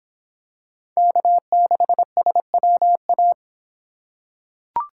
Morse code, heard as a keyed tone, about 25 wpm, 700 Hz. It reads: K6SWA